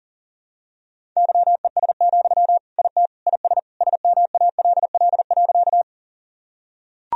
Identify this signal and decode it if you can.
Morse code, heard as a keyed tone, about 40 wpm, 700 Hz. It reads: YES, IT IS SMALL.
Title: Sentences from Top 1000 Words - Review Part 1 - 40wpm